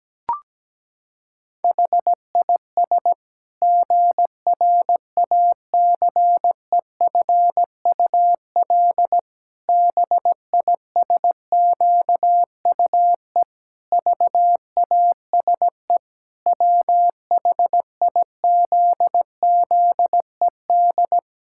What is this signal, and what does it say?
Morse code, heard as a keyed tone, about 17 wpm, 700 Hz. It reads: HIS GRACEFUL BISQUE VASE WHIZZED
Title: Pangram Challenge v2 - 17wpm